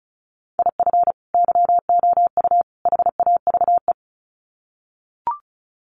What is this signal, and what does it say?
Morse code, heard as a keyed tone, about 34 wpm, 700 Hz. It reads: IF YOU HAVE
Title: Sets of 3 Words 35wpm